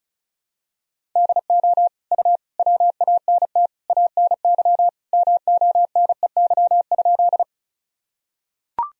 Morse code, heard as a keyed tone, about 34 wpm, 700 Hz. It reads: DO U WANT ANY MONEY?
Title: Sentences from Top 400 Words and Abbreviations 35wpm